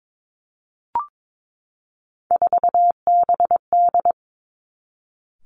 Morse code, heard as a keyed tone, about 22 wpm, 700 Hz. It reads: 4BD
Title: Three Challenging Letter-Number Combinations 22wpm